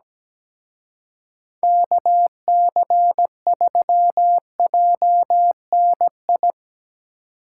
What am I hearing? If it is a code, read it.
Morse code, heard as a keyed tone, about 19 wpm, 700 Hz. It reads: KC3JNI
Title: Speed Racing - USA 2x3 Call Signs 17wpm